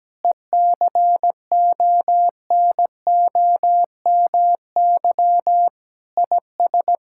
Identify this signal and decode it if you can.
Morse code, heard as a keyed tone, about 17 wpm, 700 Hz. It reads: ECONOMY IS
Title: Sentences from Top 1000 Words - Review Part 6 - 17wpm